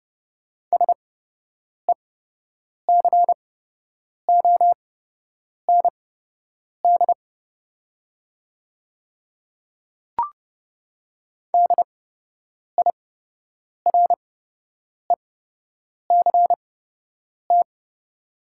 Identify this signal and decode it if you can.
Morse code, heard as a keyed tone, about 30 wpm, spaced as for 8 wpm, 700 Hz. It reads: SECOND DIRECT
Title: 8x Character Spacing - Top 500 Words - 30wpm